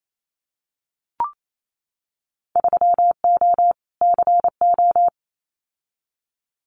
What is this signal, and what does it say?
Morse code, heard as a keyed tone, about 28 wpm, 690 Hz. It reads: 3O CO
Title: Winter Field Day Exchanges Encore 28wpm